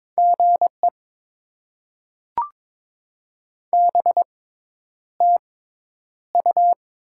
Morse code, heard as a keyed tone, about 22 wpm, spaced as for 10 wpm, 700 Hz. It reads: GE BTU
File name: Top 20 QSO Elements - 6x Character Spacing - 22wpm